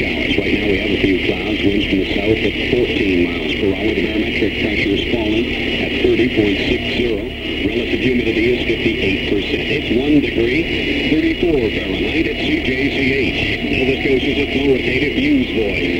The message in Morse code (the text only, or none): none